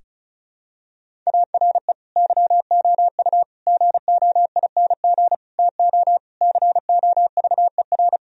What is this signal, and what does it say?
Morse code, heard as a keyed tone, about 34 wpm, 700 Hz. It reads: ARE YOU GOING TO COVER
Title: Sentences from Top 1000 Words - Review Part 1 - 35wpm